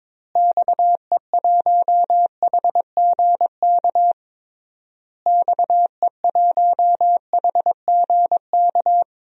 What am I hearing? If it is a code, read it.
Morse code, heard as a keyed tone, about 22 wpm, 700 Hz. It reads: XE1HGK XE1HGK
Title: DXCC-100 Call Signs - 6 Characters - Repeated 3x - 22wpm